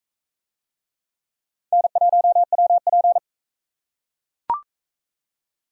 Morse code, heard as a keyed tone, about 39 wpm, 700 Hz. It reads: N1WP